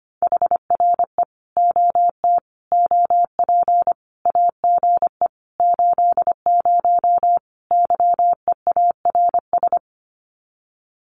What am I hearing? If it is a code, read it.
Morse code, heard as a keyed tone, about 25 wpm, 700 Hz. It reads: HRE OT OP AGE 80 YEARS
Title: Common QSO Phrases 25wpm